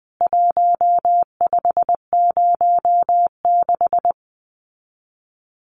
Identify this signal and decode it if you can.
Morse code, heard as a keyed tone, about 20 wpm, 700 Hz. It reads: 1506
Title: CWT Exchange 20wpm